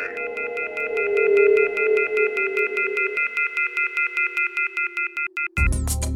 Off-hook warning tone, 1400 + 2060 + 2450 + 2600 Hz.
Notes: Mit einer Spielzeugpfeife, die er in seiner Müslipackung ( "Cap'n Crunch") fand, manipulierte er in den 70er-Jahren das weltweite Telefonnetz.